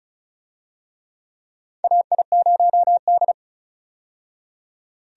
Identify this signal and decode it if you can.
Morse code, heard as a keyed tone, about 36 wpm, 700 Hz. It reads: AI0D
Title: USA Call Signs - 4 Characters - 35wpm